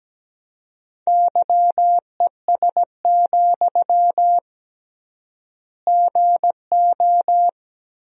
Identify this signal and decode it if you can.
Morse code, heard as a keyed tone, about 17 wpm, 700 Hz. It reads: YES, GO
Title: 3x Word-Spacing - Sentences from Top 500 Words - 17wpm